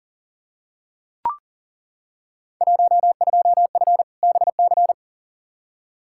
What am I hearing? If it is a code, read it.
Morse code, heard as a keyed tone, about 40 wpm, 700 Hz. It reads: 12F BC
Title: Field Day Exchanges Encore - 40wpm